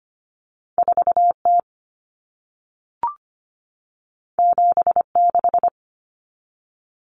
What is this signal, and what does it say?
Morse code, heard as a keyed tone, about 25 wpm, 700 Hz. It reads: 4T 76
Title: Common Cut Numbers - 2-Digit 25wpm